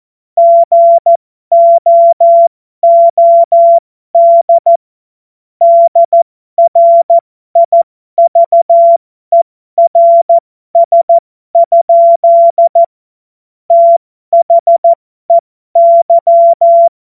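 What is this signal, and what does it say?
Morse code, heard as a keyed tone, about 14 wpm, spaced as for 12 wpm, 680 Hz. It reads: GOOD DRIVERS? THEY
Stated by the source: Morse code joke.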